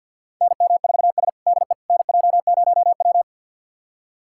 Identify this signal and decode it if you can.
Morse code, heard as a keyed tone, about 46 wpm, 700 Hz. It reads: NM4S DE NJ0W
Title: USA Callsign DE Callsign - 4 Characters - 50wpm